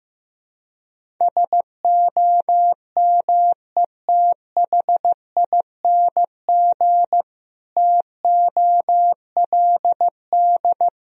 Morse code, heard as a keyed tone, about 15 wpm, 700 Hz. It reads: SOMETHING TOLD
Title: Sentences from Top 1000 Words - Review Part 1 - 15wpm